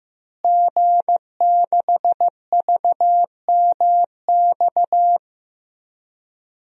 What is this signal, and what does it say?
Morse code, heard as a keyed tone, about 15 wpm, 700 Hz. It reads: G6VMX
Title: DXCC-100 Call Signs - 5 Characters - Repeated 2x - 15wpm